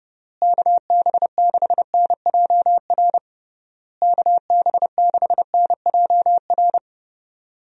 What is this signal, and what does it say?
Morse code, heard as a keyed tone, about 32 wpm, 700 Hz. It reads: KB6NJR KB6NJR
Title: USA Call Signs - 6 Characters - Repeated 3x - 30wpm